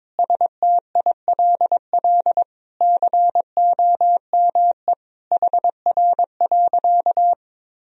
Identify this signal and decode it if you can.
Morse code, heard as a keyed tone, about 22 wpm, 700 Hz. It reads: STILL COME HR.
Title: Sentences from Top 300 Words and Abbreviations 22wpm